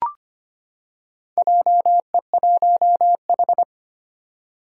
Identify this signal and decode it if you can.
Morse code, heard as a keyed tone, about 25 wpm, 700 Hz. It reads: JE1H
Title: Four Challenging Letter-Number Combinations 25wpm